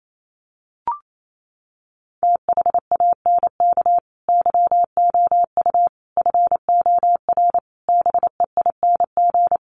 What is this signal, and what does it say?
Morse code, heard as a keyed tone, about 28 wpm, 700 Hz. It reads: THANK YOU FOR BEING
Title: Sets of 5 Words 28wpm